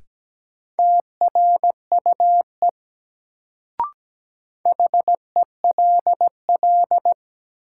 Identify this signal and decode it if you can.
Morse code, heard as a keyed tone, about 17 wpm, 700 Hz. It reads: TRUE HELL
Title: Sets of 5 Words Encore - New Word Drill - 17wpm